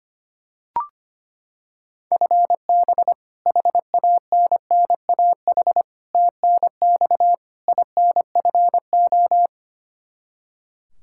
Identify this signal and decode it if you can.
Morse code, heard as a keyed tone, about 25 wpm, 700 Hz. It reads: FB HANNAH TNX INFO
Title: Practice QSO - Exchange 3 - More Info - 25wpm